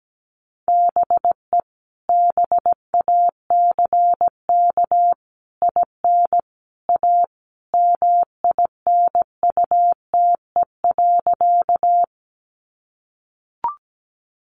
Morse code, heard as a keyed tone, about 17 wpm, 700 Hz. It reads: BE BACK IN A MINUTE.
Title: Sentences from Top 500 Words 17wpm